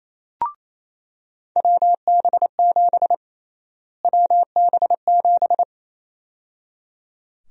Morse code, heard as a keyed tone, about 28 wpm, 700 Hz. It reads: WB7 WB7
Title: USA Call Sign Prefixes - Repeated 2x - 28wpm